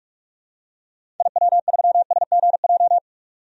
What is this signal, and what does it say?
Morse code, heard as a keyed tone, about 46 wpm, 700 Hz. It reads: IW3SGJ
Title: DXCC-100 Call Signs - 6 Characters - 45wpm